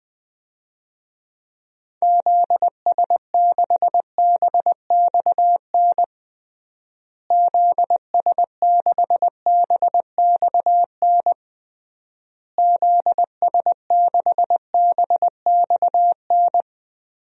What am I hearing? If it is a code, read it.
Morse code, heard as a keyed tone, about 19 wpm, 700 Hz. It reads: ZS6BXN ZS6BXN ZS6BXN
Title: DXCC-100 Call Signs - 6 Characters - Repeated 3x - 20wpm